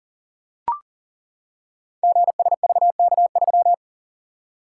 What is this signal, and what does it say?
Morse code, heard as a keyed tone, about 38 wpm, 700 Hz. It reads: GSVK3
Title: Code Groups - 5 Letter:Numbers - 40wpm